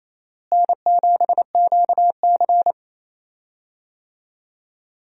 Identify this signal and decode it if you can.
Morse code, heard as a keyed tone, about 28 wpm, 700 Hz. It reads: N7QC
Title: USA Call Signs - 4 Characters - Repeated 2x - 28wpm